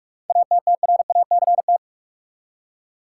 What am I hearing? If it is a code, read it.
Morse code, heard as a keyed tone, about 44 wpm, 700 Hz. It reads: ATTRACT
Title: Seven Letter Words - 45wpm